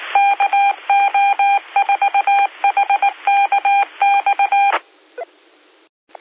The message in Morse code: KO4HKX